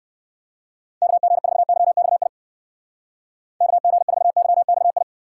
Morse code, heard as a keyed tone, about 56 wpm, 700 Hz. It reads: KG4YXI KG4YXI
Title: Speed Racing - USA 2x3 Call Signs 45wpm